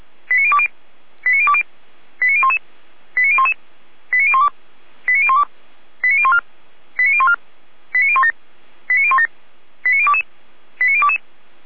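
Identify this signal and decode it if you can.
Five-tone selective call (ZVEI), heard as quick runs of five tones, tones of about 70 ms. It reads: 89020 89020 89011 89011 89012 89012 89014 89014 89017 89017 89022 89022
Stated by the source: Five-tone ZVEI selective call received on an SDR.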